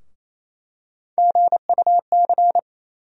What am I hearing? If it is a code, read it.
Morse code, heard as a keyed tone, about 28 wpm, 700 Hz. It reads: GUC
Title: USA Call Sign Postfixes - 3 Letters 28wpm